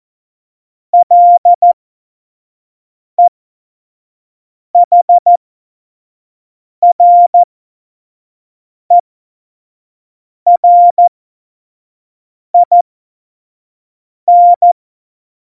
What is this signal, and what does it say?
Morse code, heard as a keyed tone, about 14 wpm, spaced as for 5 wpm, 700 Hz. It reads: LEHRERIN
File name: Morsekurs / Witze / Witz 11. Juni 2019, 25 F70